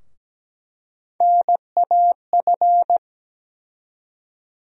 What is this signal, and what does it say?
Morse code, heard as a keyed tone, about 17 wpm, 700 Hz. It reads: NAF
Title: USA Call Sign Postfixes - 3 Letters 17wpm